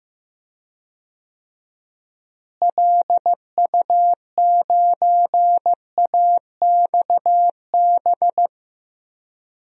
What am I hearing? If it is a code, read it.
Morse code, heard as a keyed tone, about 15 wpm, 700 Hz. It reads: LU9AXB
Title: DXCC-100 Call Signs - 6 Characters - Repeated 2x - 15wpm